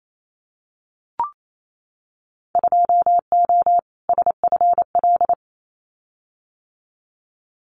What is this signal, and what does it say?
Morse code, heard as a keyed tone, about 28 wpm, 700 Hz. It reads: 2O SFL
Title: Winter Field Day Exchanges 28wpm